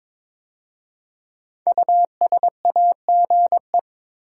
Morse code, heard as a keyed tone, about 22 wpm, 700 Hz. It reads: USAGE
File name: 5 Letter Words - First 3 Letters Given - 22wpm